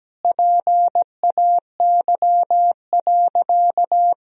Morse code, heard as a keyed tone, about 17 wpm, 700 Hz. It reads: PAY.
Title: Sentences from Top 1000 Words - Review Part 4 - 17wpm